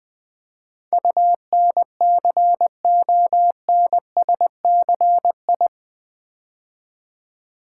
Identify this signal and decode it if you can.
Morse code, heard as a keyed tone, about 20 wpm, 700 Hz. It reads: UNCONSCI
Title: Build A Word - 4 Letters At A Time - 11-Letter Words - 20wpm